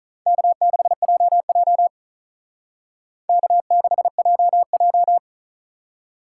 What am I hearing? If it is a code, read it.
Morse code, heard as a keyed tone, about 45 wpm, 700 Hz. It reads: K6JJ K6JJ